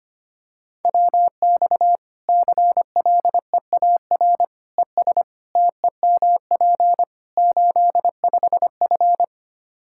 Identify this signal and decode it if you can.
Morse code, heard as a keyed tone, about 25 wpm, 700 Hz. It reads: WX CLEAR ES TEMP 85F
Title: Practice QSO - Exchange 2 - Info - 25wpm